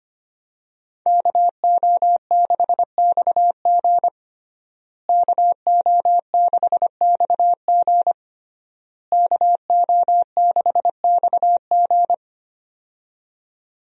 Morse code, heard as a keyed tone, about 25 wpm, 700 Hz. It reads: KO6XG KO6XG KO6XG
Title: USA Call Signs - 5 Characters - Repeated 3x - 25wpm